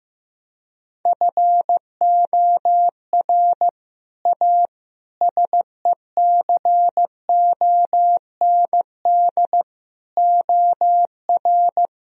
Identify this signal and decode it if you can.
Morse code, heard as a keyed tone, about 16 wpm, 700 Hz. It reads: FOR A SECOND OR
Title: Sentences from Top 1000 Words - Review Part 1 - 15wpm